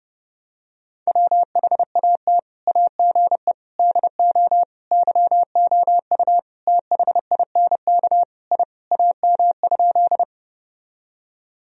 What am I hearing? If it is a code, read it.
Morse code, heard as a keyed tone, about 30 wpm, 700 Hz. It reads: WHAT AGE DO YOU THINK I AM?
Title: Sentences from Top 1000 Words - Review Part 3 - 30wpm